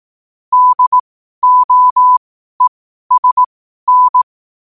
Morse code, heard as a keyed tone, about 18 wpm, spaced as for 12 wpm, 1000 Hz. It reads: DOESN
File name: qotdfiles / 12wpm